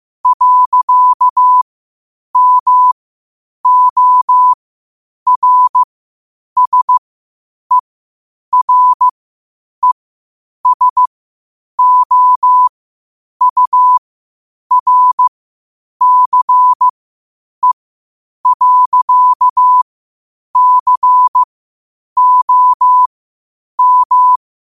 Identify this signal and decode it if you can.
Morse code, heard as a keyed tone, about 15 wpm, spaced as for 8 wpm, 1000 Hz. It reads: .MORSERESOURCE.COM
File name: qotdfiles / 8wpm